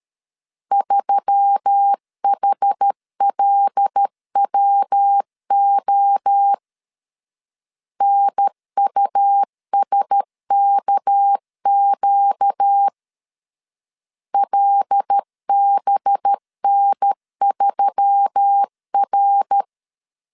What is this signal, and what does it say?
Morse code, heard as a keyed tone, about 13 wpm, 800 Hz. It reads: 3HLWO NUSKQ LBN3R
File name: LEZIONI / 6 PKX0 / 6 PKX 12wpm 2024